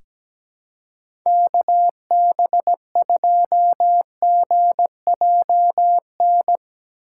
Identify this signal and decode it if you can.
Morse code, heard as a keyed tone, about 17 wpm, 700 Hz. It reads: KB2GJN
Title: USA Call Signs - 6 Characters - 3x Character Spacing - 17wpm